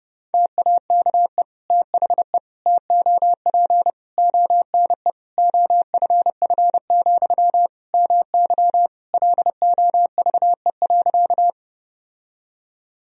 Morse code, heard as a keyed tone, about 30 wpm, 700 Hz. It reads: TAKE THE TOP ONE OFF, MY LOVE.
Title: Sentences from Top 1000 Words - Review Part 2 - 30wpm